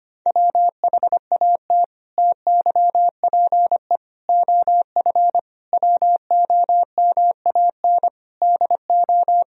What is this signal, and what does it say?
Morse code, heard as a keyed tone, about 25 wpm, 700 Hz. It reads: WHAT TYPE OF WOMAN DO
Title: Sentences from Top 1000 Words - Review Part 3 - 25wpm